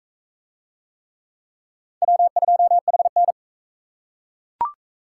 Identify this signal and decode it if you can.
Morse code, heard as a keyed tone, about 42 wpm, 700 Hz. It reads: W2HN